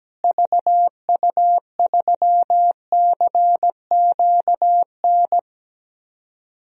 Morse code, heard as a keyed tone, about 17 wpm, 700 Hz. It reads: VU3CQN
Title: DXCC-100 Call Signs - 6 Characters - Repeated 3x - 17wpm